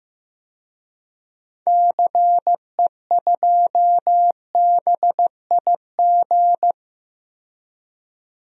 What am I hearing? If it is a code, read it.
Morse code, heard as a keyed tone, about 16 wpm, 700 Hz. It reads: CE2BIG
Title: DXCC-100 Call Signs - 6 Characters - 15wpm